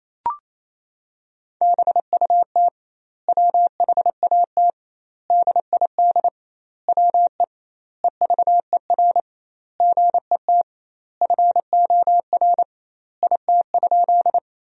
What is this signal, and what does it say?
Morse code, heard as a keyed tone, about 28 wpm, 700 Hz. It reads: BUT WHAT DID WE EVER GET FOR IT?
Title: 2x Word-Spacing - Sentences from Top 400 Words - 28wpm